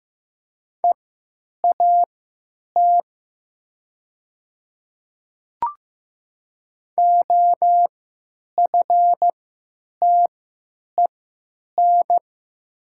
Morse code, heard as a keyed tone, about 15 wpm, spaced as for 8 wpm, 700 Hz. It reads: EAT OFTEN
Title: 3x Character Spacing - Top 300 Words - 15wpm